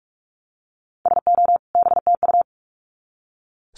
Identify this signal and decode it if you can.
Morse code, heard as a keyed tone, about 44 wpm, 700 Hz. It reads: SO BTU
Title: Common QSO Phrases 45wpm